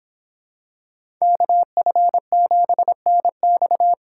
Morse code, heard as a keyed tone, about 25 wpm, 700 Hz. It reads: KF7NX